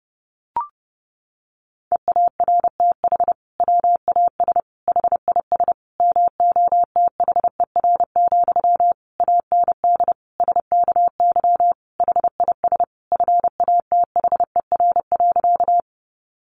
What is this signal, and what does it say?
Morse code, heard as a keyed tone, about 30 wpm, 700 Hz. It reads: EARTH WAS HIS MOTHER, AND SKY HIS FATHER.
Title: Sentences from Top 1200 Words 30wpm